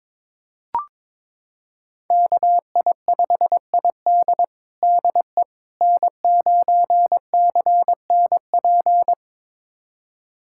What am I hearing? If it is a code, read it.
Morse code, heard as a keyed tone, about 22 wpm, 700 Hz. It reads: KI5ID DE N9CNP
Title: USA Callsign DE Callsign - 5 Characters - 22wpm